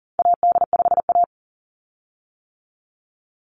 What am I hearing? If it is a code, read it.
Morse code, heard as a keyed tone, about 41 wpm, 700 Hz. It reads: AD5A
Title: Call Signs - 40wpm